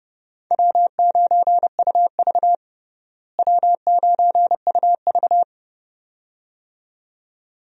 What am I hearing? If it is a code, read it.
Morse code, heard as a keyed tone, about 30 wpm, 700 Hz. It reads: W9UV W9UV
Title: USA Call Signs - 4 Characters - Repeated 2x - 30wpm